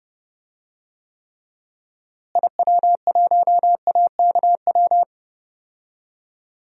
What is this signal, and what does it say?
Morse code, heard as a keyed tone, about 32 wpm, 700 Hz. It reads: IW1AKW
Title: DXCC-100 Call Signs - 6 Characters - Repeated 3x - 30wpm